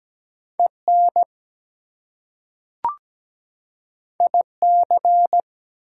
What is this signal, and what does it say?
Morse code, heard as a keyed tone, about 17 wpm, 700 Hz. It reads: EN IC
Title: Prefixes and Suffixes 17wpm